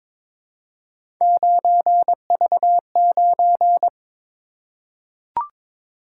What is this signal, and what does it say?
Morse code, heard as a keyed tone, about 22 wpm, 700 Hz. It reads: 9V9
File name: practice / 22wpm / Three Letter-Number Combinations - 22wpm